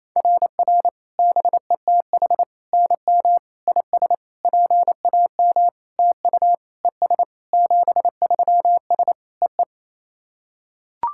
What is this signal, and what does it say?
Morse code, heard as a keyed tone, about 27 wpm, 690 Hz. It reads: RR BETH NM IS PAM TU ES 73S EE
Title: SOTA QSO Practice 28wpm